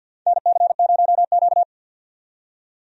Morse code, heard as a keyed tone, about 48 wpm, 700 Hz. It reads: NC0Q
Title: USA Call Signs - 4 Characters - 50wpm